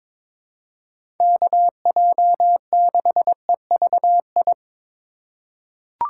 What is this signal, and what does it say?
Morse code, heard as a keyed tone, about 22 wpm, 700 Hz. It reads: KJ6EVI
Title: USA Call Signs - 6 Characters - 2x Character Spacing - 22wpm